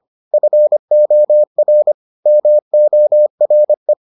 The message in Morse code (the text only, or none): FOR MORE